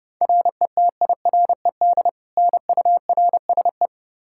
Morse code, heard as a keyed tone, about 30 wpm, 700 Hz. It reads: RETIRED NURSE
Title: K1USN SST QSO Practice 30wpm